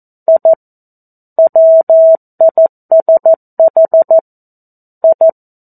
Morse code, heard as a keyed tone, about 14 wpm, 650 Hz. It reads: I WISH I
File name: 02-Chapter-02 14 WPM